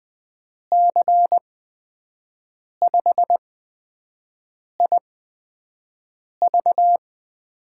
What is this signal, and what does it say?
Morse code, heard as a keyed tone, about 20 wpm, spaced as for 5 wpm, 700 Hz. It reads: C5IV
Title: USA Call Signs - 5 Characters - 8x Character Spacing - 20wpm